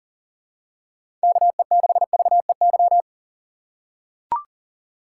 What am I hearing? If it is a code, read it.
Morse code, heard as a keyed tone, about 40 wpm, 700 Hz. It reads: KE6VEY
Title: USA Call Signs - 6 Characters - 6x Character Spacing - 40wpm